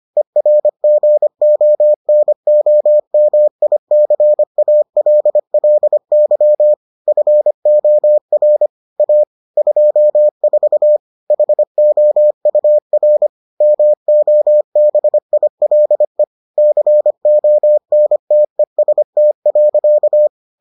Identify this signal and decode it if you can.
Morse code, heard as a keyed tone, about 25 wpm, 600 Hz. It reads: ERGONOMICALLY FOR A 24 HOUR MOBILE CONTEST.